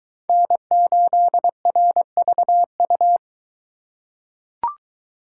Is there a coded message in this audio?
Yes: Morse code from 0.3 to 3.2 s, heard as a keyed tone, about 23 wpm, 700 Hz. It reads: N8RVU